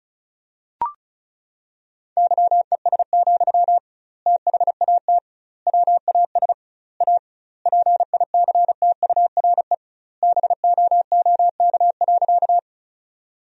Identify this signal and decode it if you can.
Morse code, heard as a keyed tone, about 34 wpm, 700 Hz. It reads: YES, THAT WAS A PICTURE BOOK.
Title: 2x Word-Spacing - Sentences from Top 500 Words - 35wpm